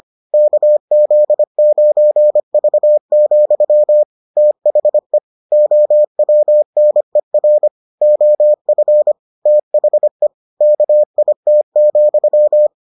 Morse code, heard as a keyed tone, about 25 wpm, 600 Hz. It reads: KZ9V, THE OWNER OF THE KIT,